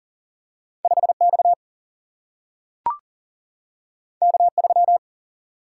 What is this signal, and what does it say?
Morse code, heard as a keyed tone, about 40 wpm, 700 Hz. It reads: 5X K3
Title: Single-Letter Single-Number 40wpm